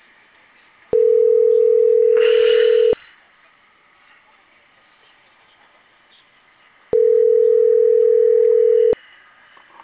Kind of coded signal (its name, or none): telephone call progress tone